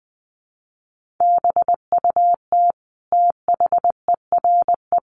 Morse code, heard as a keyed tone, about 20 wpm, 700 Hz. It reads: BUT THERE
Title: Sets of 5 Words Encore2 20wpm